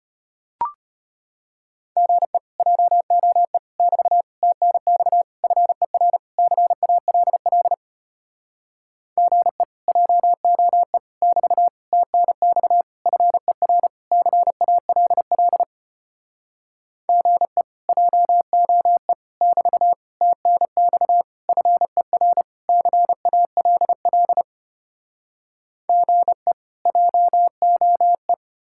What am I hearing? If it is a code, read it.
Morse code, heard as a keyed tone, about 33 wpm, 700 Hz. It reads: GE JOE <BT> TNX FER CALL GE JOE <BT> TNX FER CALL GE JOE <BT> TNX FER CALL GE JOE